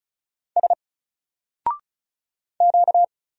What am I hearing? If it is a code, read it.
Morse code, heard as a keyed tone, about 35 wpm, 700 Hz. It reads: S Q
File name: Single-Letter-Number-35wpm